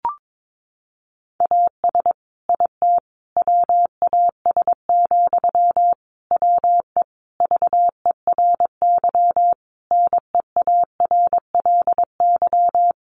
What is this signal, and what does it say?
Morse code, heard as a keyed tone, about 22 wpm, 700 Hz. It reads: AS IT WAS, WE VERY NEARLY
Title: Sentences from Top 1400 Words 22wpm